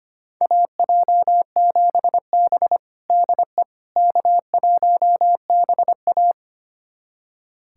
Morse code, heard as a keyed tone, about 25 wpm, 700 Hz. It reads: AJ7B DE K1BA
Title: USA Callsign DE Callsign - 4 Characters - 25wpm